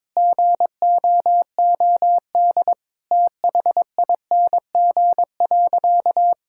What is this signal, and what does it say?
Morse code, heard as a keyed tone, about 22 wpm, 700 Hz. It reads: GOOD THING.
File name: practice / 22wpm / Sentences with 3-Word Phrases - 22wpm